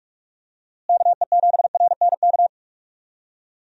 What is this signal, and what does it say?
Morse code, heard as a keyed tone, about 46 wpm, 700 Hz. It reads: KE7RNK
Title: USA Call Signs - 6 Characters - Repeated 2x - 45wpm